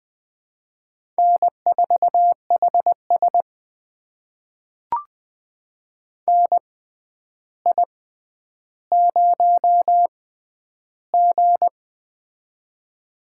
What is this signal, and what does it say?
Morse code, heard as a keyed tone, about 20 wpm, spaced as for 10 wpm, 700 Hz. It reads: N4HS NI0G
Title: USA Call Signs - 4 Characters - 6x Character Spacing - 20wpm